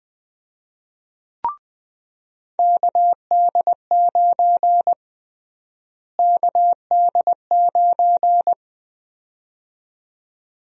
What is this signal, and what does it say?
Morse code, heard as a keyed tone, about 21 wpm, 700 Hz. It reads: KD9 KD9
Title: USA Call Sign Prefixes - Repeated 2x - 20wpm